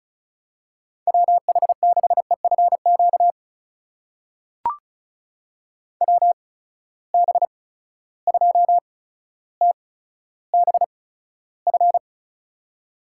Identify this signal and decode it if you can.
Morse code, heard as a keyed tone, about 36 wpm, spaced as for 15 wpm, 700 Hz. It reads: WH6EFQ WB2TBF
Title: USA Call Signs - 8x Character Spacing - 35wpm